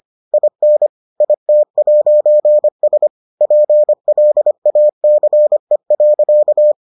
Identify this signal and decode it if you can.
Morse code, heard as a keyed tone, about 25 wpm, 600 Hz. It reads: IN IT'S PLACE.